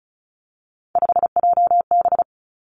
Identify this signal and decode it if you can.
Morse code, heard as a keyed tone, about 36 wpm, 700 Hz. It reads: 5JB
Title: Three Challenging Letter-Number Combinations 35wpm